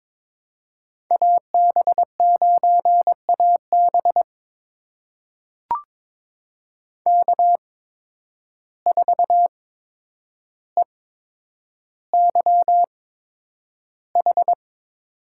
Morse code, heard as a keyed tone, about 22 wpm, spaced as for 9 wpm, 700 Hz. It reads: AB9AB K4EYH
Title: USA Call Signs - 5 Characters - 8x Character Spacing - 22wpm